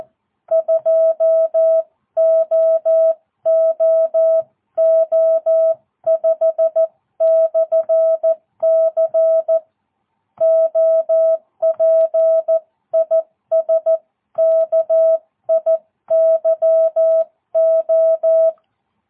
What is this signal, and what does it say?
Morse code, heard as a keyed tone, about 14 wpm, 650 Hz. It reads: 2OOO5/C OP IS KIYO
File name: jh8cbh / 2022cw